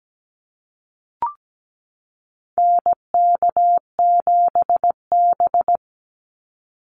Morse code, heard as a keyed tone, about 17 wpm, 700 Hz. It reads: NK7B
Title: Call Signs Repeated 3x - 17wpm